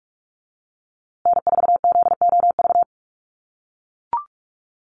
Morse code, heard as a keyed tone, about 44 wpm, 700 Hz. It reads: N4ZOV
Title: USA Call Signs Repeated 3x 45wpm